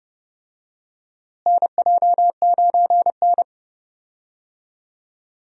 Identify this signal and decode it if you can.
Morse code, heard as a keyed tone, about 31 wpm, 700 Hz. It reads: NJ9N
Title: USA Call Signs - 4 Characters - Repeated 2x - 30wpm